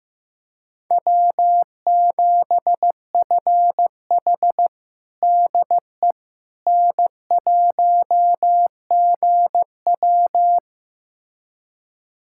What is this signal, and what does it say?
Morse code, heard as a keyed tone, about 15 wpm, 700 Hz. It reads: W7FH DE N1GW
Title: USA Callsign DE Callsign - 4 Characters - 15wpm